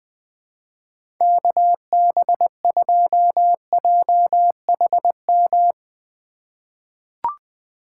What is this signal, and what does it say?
Morse code, heard as a keyed tone, about 20 wpm, 700 Hz. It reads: KB2JHM
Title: USA Call Signs - 6 Characters - 3x Character Spacing - 20wpm